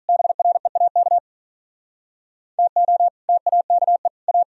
Morse code, heard as a keyed tone, about 45 wpm, 700 Hz. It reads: BREAK TO TAKE A